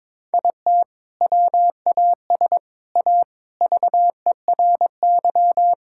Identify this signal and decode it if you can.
Morse code, heard as a keyed tone, about 22 wpm, 700 Hz. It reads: IT WAS A VERY
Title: Sentences from Top 1000 Words - Review Part 5 - 22wpm